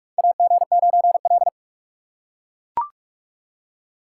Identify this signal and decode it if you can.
Morse code, heard as a keyed tone, about 41 wpm, 690 Hz. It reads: AG9L
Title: USA Call Signs - 4 Characters - 8x Character Spacing - 45wpm